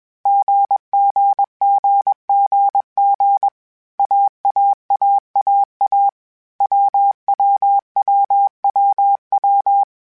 Morse code, heard as a keyed tone, about 21 wpm, 800 Hz. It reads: GGGGG AAAAA WWWWW
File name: Groups@20 / Groups@20 mp3s / AWUNDG@20